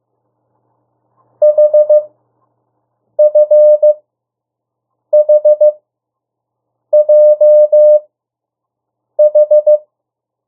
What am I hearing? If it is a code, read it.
Morse code, heard as a keyed tone, about 15 wpm, spaced as for 6 wpm, 600 Hz. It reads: HFHJH
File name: urok 1